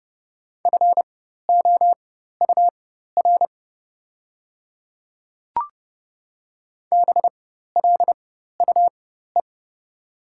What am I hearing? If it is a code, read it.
Morse code, heard as a keyed tone, about 30 wpm, spaced as for 14 wpm, 700 Hz. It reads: FOUR BLUE
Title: 4x Character Spacing - Top 500 Words - 30wpm